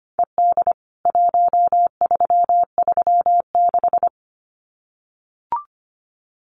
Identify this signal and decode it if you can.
Morse code, heard as a keyed tone, about 25 wpm, 700 Hz. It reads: ED 1336
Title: CWT Exchange 25wpm